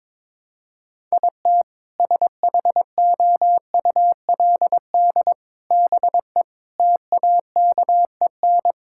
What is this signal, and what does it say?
Morse code, heard as a keyed tone, about 22 wpm, 700 Hz. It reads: IT SHOULD BE TAKEN
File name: practice / 22wpm / Sentences with 8-Word Phrases - 22wpm